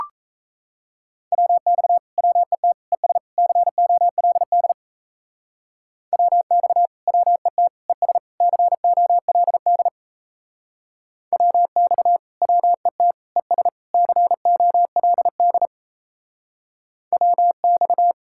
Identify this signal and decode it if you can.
Morse code, heard as a keyed tone, about 38 wpm, 700 Hz. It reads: WX WET ES COLD WX WET ES COLD WX WET ES COLD WX